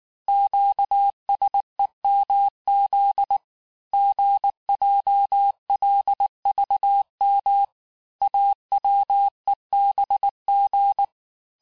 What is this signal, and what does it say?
Morse code, heard as a keyed tone, about 19 wpm, 800 Hz. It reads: QSEMZ GJLVM AWEBG
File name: cours-du-mercredi / audios / cours-869-texte-CHIFFRE-vitesse-18-MPM